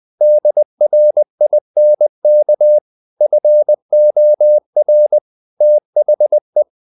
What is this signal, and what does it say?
Morse code, heard as a keyed tone, about 20 wpm, 600 Hz. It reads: DRINK FOR THE
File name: the old man and the sea part 2 20wpm